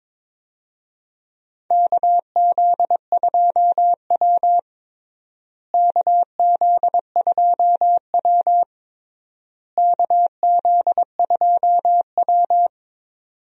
Morse code, heard as a keyed tone, about 22 wpm, 700 Hz. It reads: KZ2W KZ2W KZ2W
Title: USA Call Signs - 4 Characters - Repeated 3x - 22wpm